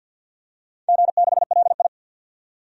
Morse code, heard as a keyed tone, about 52 wpm, 700 Hz. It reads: G6LI
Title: DXCC-100 Call Signs - 4 Characters - 50wpm